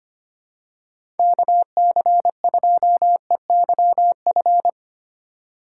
Morse code, heard as a keyed tone, about 25 wpm, 700 Hz. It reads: KC2EYF
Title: USA Call Signs - 6 Characters - 3x Character Spacing - 25wpm